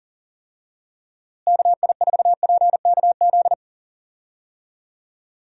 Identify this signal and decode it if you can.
Morse code, heard as a keyed tone, about 39 wpm, 700 Hz. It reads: KI4PKZ
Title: USA Call Signs - 6 Characters - Repeated 3x - 40wpm